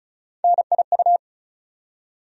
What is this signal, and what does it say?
Morse code, heard as a keyed tone, about 36 wpm, 700 Hz. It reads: NIU
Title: USA Call Sign Postfixes - 3 Letters 35wpm